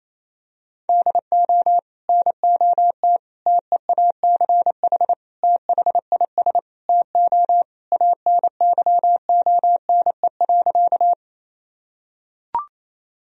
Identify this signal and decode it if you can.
Morse code, heard as a keyed tone, about 28 wpm, 700 Hz. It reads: DO NOT TEACH THIS TO ANYONE.
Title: Sentences from Top 1000 Words - Review Part 2 - 28wpm